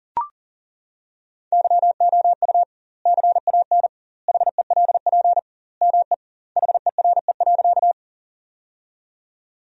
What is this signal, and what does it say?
Morse code, heard as a keyed tone, about 40 wpm, 700 Hz. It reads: YOU CAN HELP ME HERE.
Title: 2x Word-Spacing - Sentences from Top 200 Words - 40wpm